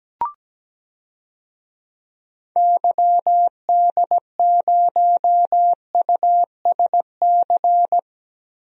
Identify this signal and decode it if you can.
Morse code, heard as a keyed tone, about 17 wpm, 700 Hz. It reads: YD0USC
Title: DXCC-100 Call Signs - 6 Characters - Repeated 2x - 17wpm